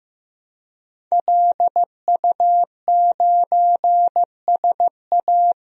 Morse code, heard as a keyed tone, about 15 wpm, 700 Hz. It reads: LU9SA
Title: DXCC-100 Call Signs - 5 Characters - 15wpm